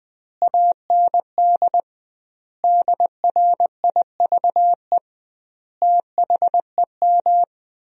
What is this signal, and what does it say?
Morse code, heard as a keyed tone, about 20 wpm, 700 Hz. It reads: AND DRIVE THEM
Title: 2x Word-Spacing - Sentences from Top 500 Words - 20wpm